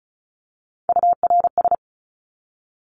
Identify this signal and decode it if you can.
Morse code, heard as a keyed tone, about 34 wpm, 700 Hz. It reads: URS
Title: Abbreviations 35wpm